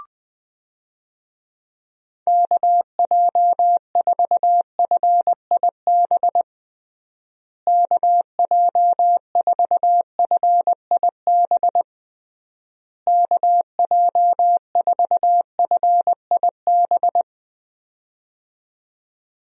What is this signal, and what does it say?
Morse code, heard as a keyed tone, about 20 wpm, 700 Hz. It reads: KJ4FIB KJ4FIB KJ4FIB
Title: USA Call Signs - 6 Characters - Repeated 3x - 20wpm